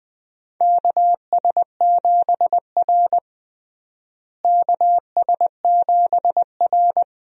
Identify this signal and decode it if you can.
Morse code, heard as a keyed tone, about 20 wpm, 700 Hz. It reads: KS7R KS7R
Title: USA Call Signs - 4 Characters - Repeated 3x - 20wpm